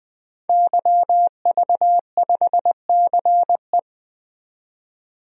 Morse code, heard as a keyed tone, about 20 wpm, 700 Hz. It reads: YV5CE
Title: DXCC-100 Call Signs - 5 Characters - 20wpm